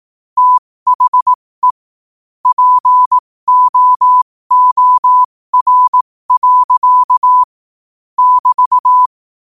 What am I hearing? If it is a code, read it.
Morse code, heard as a keyed tone, about 18 wpm, spaced as for 15 wpm, 1000 Hz. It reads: THE POOR. <BT>
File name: qotdfiles / 15wpm